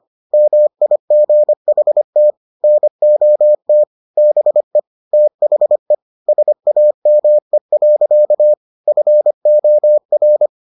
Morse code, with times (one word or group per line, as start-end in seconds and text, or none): 0.3-2.3 MIGHT
2.6-3.8 NOT
4.2-4.8 BE
5.1-5.9 THE
6.3-8.5 SAME.
8.9-10.5 FOR